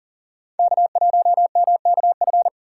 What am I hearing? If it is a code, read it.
Morse code, heard as a keyed tone, about 40 wpm, 700 Hz. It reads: K1MKF
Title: USA Call Signs - 5 Characters - 2x Character Spacing - 40wpm